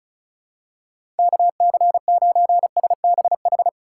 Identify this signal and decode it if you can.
Morse code, heard as a keyed tone, about 36 wpm, 700 Hz. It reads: KC9SBH
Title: USA Call Signs - 6 Characters - 4x Character Spacing - 35wpm